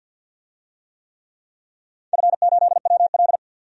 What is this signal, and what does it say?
Morse code, heard as a keyed tone, about 46 wpm, 700 Hz. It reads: F8WL
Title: DXCC-100 Call Signs - 4 Characters - 50wpm